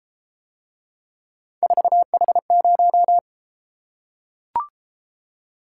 Morse code, heard as a keyed tone, about 33 wpm, 700 Hz. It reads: 4H0